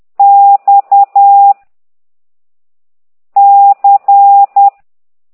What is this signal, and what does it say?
Morse code, heard as a keyed tone, about 10 wpm, 800 Hz. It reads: X C